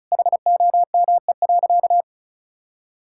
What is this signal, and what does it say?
Morse code, heard as a keyed tone, about 36 wpm, 700 Hz. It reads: HOME.
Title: Sentences from Top 1600 Words 35wpm